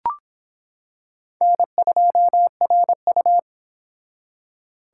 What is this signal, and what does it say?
Morse code, heard as a keyed tone, about 25 wpm, 700 Hz. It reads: N2RU